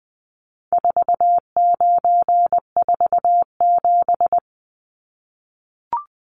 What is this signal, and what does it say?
Morse code, heard as a keyed tone, about 20 wpm, 700 Hz. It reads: 4947
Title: Numbers - 4-Digit 20wpm